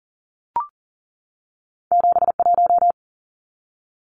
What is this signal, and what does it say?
Morse code, heard as a keyed tone, about 41 wpm, 700 Hz. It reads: 71
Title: Numbers - 2-Digit 40wpm